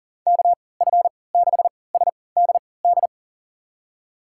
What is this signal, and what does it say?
Morse code, heard as a keyed tone, about 38 wpm, spaced as for 22 wpm, 700 Hz. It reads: KF6SDD
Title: USA Call Signs - 6 Characters - 3x Character Spacing - 40wpm